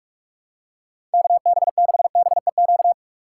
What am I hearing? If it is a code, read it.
Morse code, heard as a keyed tone, about 46 wpm, 700 Hz. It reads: KB6BEQ
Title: USA Call Signs - 6 Characters - 4x Character Spacing - 45wpm